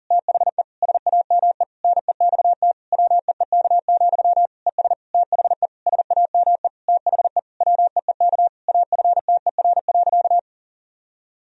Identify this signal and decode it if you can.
Morse code, heard as a keyed tone, about 40 wpm, 700 Hz. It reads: THE SAME NEXT WEEK, ES THE SAME THE WEEK AFTER.
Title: Sentences from Top 500 Words and Abbreviations 40wpm